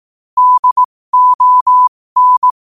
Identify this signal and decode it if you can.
Morse code, heard as a keyed tone, about 18 wpm, spaced as for 15 wpm, 1000 Hz. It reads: DON
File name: qotdfiles / 15wpm